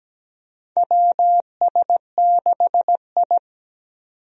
Morse code, heard as a keyed tone, about 17 wpm, 700 Hz. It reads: WS6I
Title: USA Call Signs - 4 Characters - 17wpm